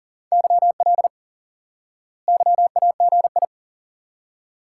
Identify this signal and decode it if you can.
Morse code, heard as a keyed tone, about 40 wpm, 700 Hz. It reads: YL YAGI
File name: IQE Course 3 - Lesson 89 - YAGI - Two Elements - 40wpm